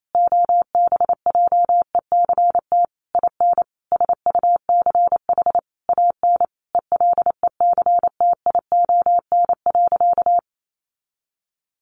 Morse code, heard as a keyed tone, about 28 wpm, 700 Hz. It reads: OBJECT IN SUCH AN ELECTION.
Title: Sentences from Top 1500 Words 28wpm